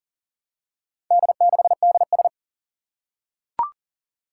Morse code, heard as a keyed tone, about 40 wpm, 700 Hz. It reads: D6DS
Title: Four Challenging Letter-Number Combinations 40wpm